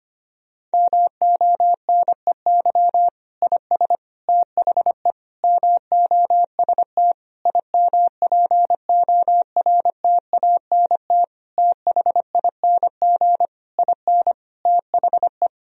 Morse code, heard as a keyed tone, about 25 wpm, 700 Hz. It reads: MONEY IS THE MOST IMPORTANT THING IN THE
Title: Sentences from Top 1000 Words - Review Part 1 - 25wpm